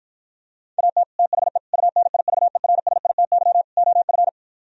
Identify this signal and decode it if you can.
Morse code, heard as a keyed tone, about 53 wpm, 700 Hz. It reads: AT THE UNIVERSITY OF